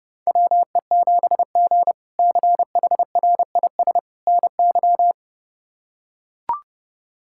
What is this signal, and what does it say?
Morse code, heard as a keyed tone, about 30 wpm, 700 Hz. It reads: WE7G CHRIS NY
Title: K1USN SST QSO Practice - 30wpm